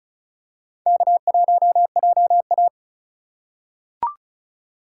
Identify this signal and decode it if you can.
Morse code, heard as a keyed tone, about 36 wpm, 700 Hz. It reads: K1JA
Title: USA Call Signs - 4 Characters - 2x Character Spacing - 35wpm